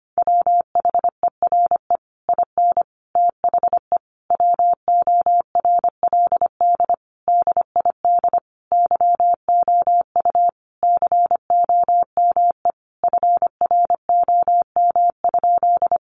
Morse code, heard as a keyed tone, about 25 wpm, 700 Hz. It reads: WHERE IN THE WORLD DID YOU COME FROM?
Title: Sentences from Top 200 Words 25wpm